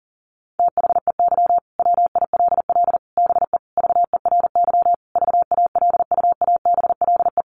Morse code, heard as a keyed tone, about 40 wpm, 700 Hz. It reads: THEY WILL BE VERY VALUABLE
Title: Sentences from Top 1100 Words 40wpm